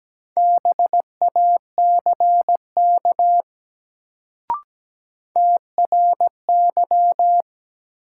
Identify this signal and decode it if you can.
Morse code, heard as a keyed tone, about 17 wpm, 700 Hz. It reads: BACK TRY
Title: Sets of 2 Words - New Word Drill - 17wpm